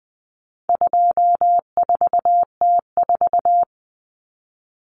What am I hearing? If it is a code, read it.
Morse code, heard as a keyed tone, about 20 wpm, 700 Hz. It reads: 24T4
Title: Common Cut Numbers - 4-Digit 20wpm